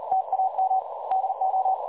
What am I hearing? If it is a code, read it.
Morse code, heard as a keyed tone, about 39 wpm, 760 Hz, with noise. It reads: TAM DO
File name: dlazaawansowanych / nagraniazpasma / websdr recording 3565.0kHz